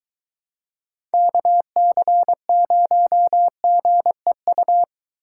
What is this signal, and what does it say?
Morse code, heard as a keyed tone, about 22 wpm, 700 Hz. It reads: KC0GEU